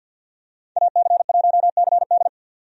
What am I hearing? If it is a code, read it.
Morse code, heard as a keyed tone, about 47 wpm, 700 Hz. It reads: AC1CD
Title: USA Call Signs - 5 Characters - 50wpm